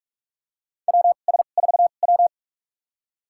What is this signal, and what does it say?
Morse code, heard as a keyed tone, about 46 wpm, spaced as for 32 wpm, 700 Hz. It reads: WS4W
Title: USA Call Signs - 4 Characters - 2x Character Spacing - 45wpm